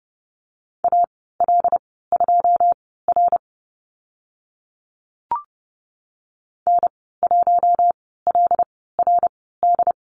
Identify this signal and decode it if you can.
Morse code, heard as a keyed tone, about 30 wpm, spaced as for 17 wpm, 700 Hz. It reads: AL2R N1LRD
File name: Lesson 14g - Number 2 - Call Signs 30wpm